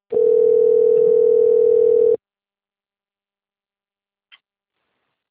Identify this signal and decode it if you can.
Ringback tone (North America), 440 + 480 Hz.